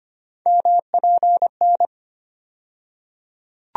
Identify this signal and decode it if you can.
Morse code, heard as a keyed tone, about 25 wpm, 700 Hz. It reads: MPN
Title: USA Call Sign Postfixes - 3 Letters 25wpm